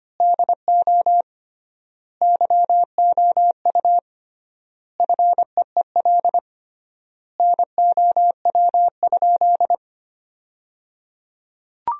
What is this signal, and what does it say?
Morse code, heard as a keyed tone, about 25 wpm, 700 Hz. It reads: DO YOU FEEL NOW?
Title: 3x Word-Spacing - Sentences from Top 400 Words - 25wpm